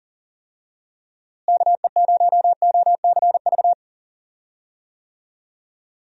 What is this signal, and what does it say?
Morse code, heard as a keyed tone, about 38 wpm, 700 Hz. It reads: KE0OCV
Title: USA Call Signs - 6 Characters - Repeated 2x - 40wpm